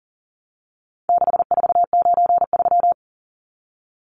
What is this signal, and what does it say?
Morse code, heard as a keyed tone, about 40 wpm, 700 Hz. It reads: 6493
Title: Numbers - 4-Digit 40wpm